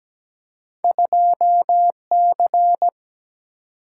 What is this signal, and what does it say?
Morse code, heard as a keyed tone, about 17 wpm, 700 Hz. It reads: 2C
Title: Single-Letter Single-Number 17wpm